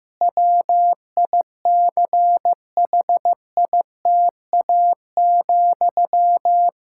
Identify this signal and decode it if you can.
Morse code, heard as a keyed tone, about 15 wpm, 700 Hz. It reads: WICHITA,
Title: Practice QSO - Exchange 1 - Beginning - 15wpm